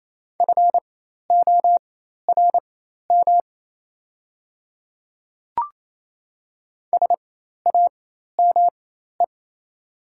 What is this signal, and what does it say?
Morse code, heard as a keyed tone, about 28 wpm, spaced as for 13 wpm, 700 Hz. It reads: FORM SAME
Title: 4x Character Spacing - Top 300 Words - 28wpm